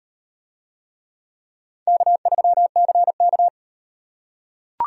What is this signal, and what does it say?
Morse code, heard as a keyed tone, about 33 wpm, 700 Hz. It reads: K3CK